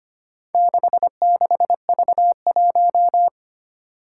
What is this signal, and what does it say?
Morse code, heard as a keyed tone, about 25 wpm, 700 Hz. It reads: 66V1
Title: Four Challenging Letter-Number Combinations 25wpm